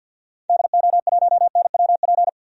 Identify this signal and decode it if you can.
Morse code, heard as a keyed tone, about 52 wpm, 700 Hz. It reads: DO1NWP
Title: DXCC-100 Call Signs - 6 Characters - 50wpm